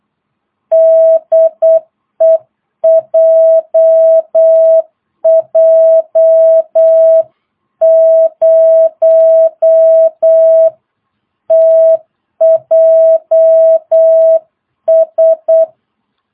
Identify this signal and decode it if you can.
Morse code, heard as a keyed tone, about 8 wpm, 660 Hz. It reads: DE JJ0TJS
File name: jh8cbh / 2022cw